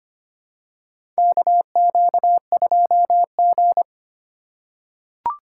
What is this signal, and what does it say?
Morse code, heard as a keyed tone, about 25 wpm, 700 Hz. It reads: KQ2G
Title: USA Call Signs - 4 Characters - 3x Character Spacing - 25wpm